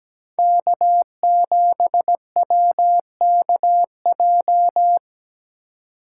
Morse code, heard as a keyed tone, about 17 wpm, 700 Hz. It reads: K7WKJ
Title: USA Call Signs - 4x Character Spacing - 17wpm